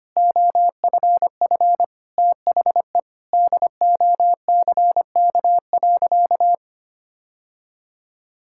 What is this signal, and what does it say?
Morse code, heard as a keyed tone, about 25 wpm, 700 Hz. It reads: OFF THE DOCK.
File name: practice / 25wpm / Sentences with 2-Word Phrases - 25wpm